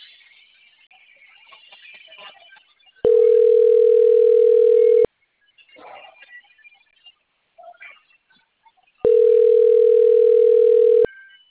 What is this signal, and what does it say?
Ringback tone (North America), 440 + 480 Hz.
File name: out- - - - - .9722